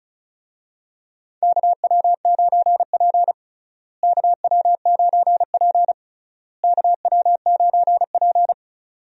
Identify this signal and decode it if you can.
Morse code, heard as a keyed tone, about 36 wpm, 700 Hz. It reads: KW9P KW9P KW9P
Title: USA Call Signs - 4 Characters - Repeated 3x - 35wpm